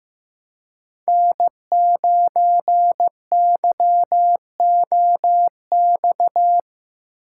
Morse code, heard as a keyed tone, about 15 wpm, 700 Hz. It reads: N9YOX
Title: USA Call Signs - 5 Characters - 4x Character Spacing - 15wpm